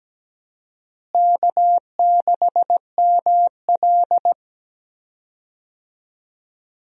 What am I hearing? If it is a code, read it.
Morse code, heard as a keyed tone, about 17 wpm, 700 Hz. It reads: K6ML
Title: USA Call Signs - 4 Characters - Repeated 2x - 17wpm